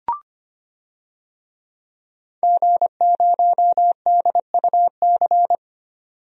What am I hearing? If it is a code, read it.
Morse code, heard as a keyed tone, about 25 wpm, 700 Hz. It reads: G0DUC
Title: DXCC-100 Call Signs - 5 Characters - 25wpm